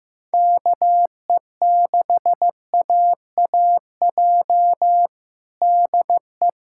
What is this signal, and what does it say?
Morse code, heard as a keyed tone, about 16 wpm, 700 Hz. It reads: KE6AAJ DE
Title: USA Callsign DE Callsign - 6 Characters - 15wpm